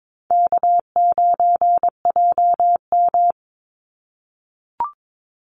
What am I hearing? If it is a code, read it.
Morse code, heard as a keyed tone, about 22 wpm, 700 Hz. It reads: K9JM
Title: Easy Call Signs - 22wpm